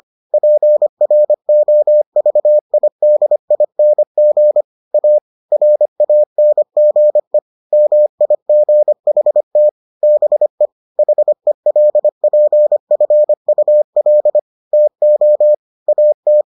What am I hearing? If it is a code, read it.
Morse code, heard as a keyed tone, about 25 wpm, 600 Hz. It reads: PROVIDING A RANGE MIGHT BE HELPFUL TO AT